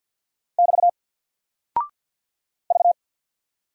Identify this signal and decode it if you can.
Morse code, heard as a keyed tone, about 49 wpm, 700 Hz. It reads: <BT> V
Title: Single Character (Letter:Number:Punctuation:Prosign) 50wpm